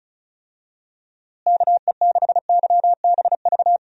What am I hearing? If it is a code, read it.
Morse code, heard as a keyed tone, about 36 wpm, 700 Hz. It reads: KE6YBV
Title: USA Call Signs - 6 Characters - Repeated 3x - 35wpm